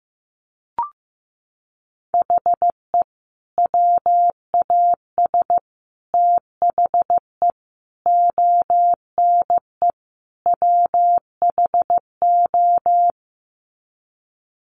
Morse code, heard as a keyed tone, about 15 wpm, 700 Hz. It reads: HE WAS THE ONE WHO
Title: Sets of 5 Words Encore2 15wpm